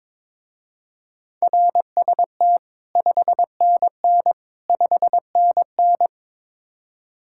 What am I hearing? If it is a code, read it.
Morse code, heard as a keyed tone, about 22 wpm, 700 Hz. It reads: RST 5NN 5NN
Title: Speed-Racing Common QSO Phrases - 22wpm